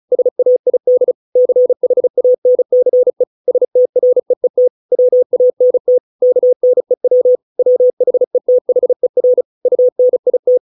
Morse code, heard as a keyed tone, about 34 wpm, 500 Hz. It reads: SAID CHANCE STREET WANT KNEW WHETHER UNIT